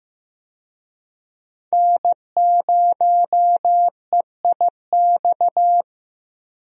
Morse code, heard as a keyed tone, about 15 wpm, 700 Hz. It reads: N0EIX
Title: USA Call Signs - 5 Characters - Repeated 3x - 15wpm